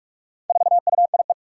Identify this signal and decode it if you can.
Morse code, heard as a keyed tone, about 41 wpm, 700 Hz. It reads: 4UIE
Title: Four Challenging Letter-Number Combinations 45wpm